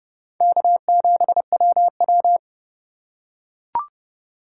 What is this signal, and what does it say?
Morse code, heard as a keyed tone, about 28 wpm, 700 Hz. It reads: K7WW